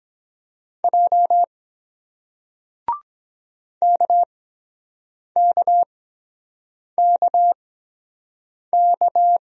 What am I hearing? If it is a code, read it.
Morse code, heard as a keyed tone, about 22 wpm, spaced as for 7 wpm, 700 Hz. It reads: J KKKK